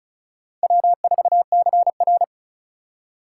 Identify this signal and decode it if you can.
Morse code, heard as a keyed tone, about 36 wpm, 700 Hz. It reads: W4CR
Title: USA Call Signs - 4 Characters - 4x Character Spacing - 35wpm